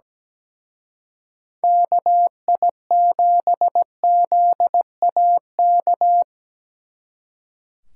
Morse code, heard as a keyed tone, about 20 wpm, 700 Hz. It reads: KI7ZAK
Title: Speed Racing - USA 2x3 Call Signs 17wpm